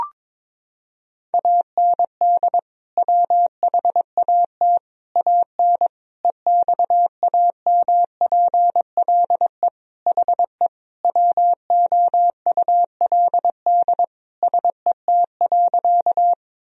Morse code, heard as a keyed tone, about 22 wpm, 700 Hz. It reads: AND WHAT AN EXAMPLE HE WOULD SET.
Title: Sentences from Top 1000 Words - Review Part 1 - 22wpm